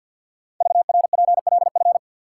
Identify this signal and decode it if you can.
Morse code, heard as a keyed tone, about 48 wpm, 700 Hz. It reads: VRPLF
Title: Code Groups - 5 Letters - 50wpm